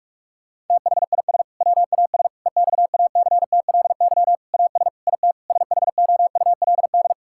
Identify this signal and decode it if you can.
Morse code, heard as a keyed tone, about 43 wpm, 700 Hz. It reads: THIS WAS EXACTLY AS IT SHOULD
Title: Sentences from Top 1000 Words - Review Part 6 - 45wpm